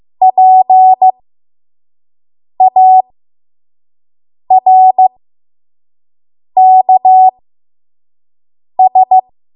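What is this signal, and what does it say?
Morse code, heard as a keyed tone, about 15 wpm, spaced as for 5 wpm, 750 Hz. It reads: PARKS
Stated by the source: 5 WPM Morse code practice.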